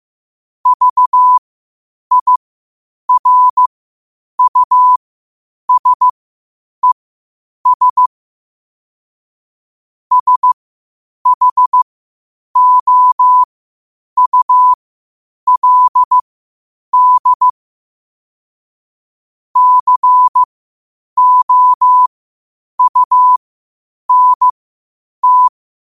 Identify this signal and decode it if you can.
Morse code, heard as a keyed tone, about 15 wpm, spaced as for 8 wpm, 1000 Hz. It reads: VIRUSES SHOULD COUNT
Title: 8 WPM morse code quotes for Fri, 01 Mar 2024